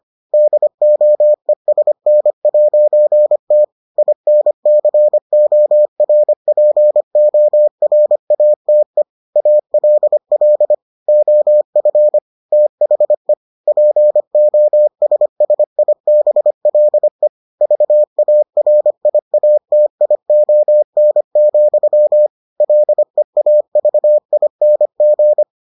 Morse code, heard as a keyed tone, about 25 wpm, 600 Hz. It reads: DOESN'T INCORPORATE ALL OF THE POSSIBLE VARIATION, LEAVING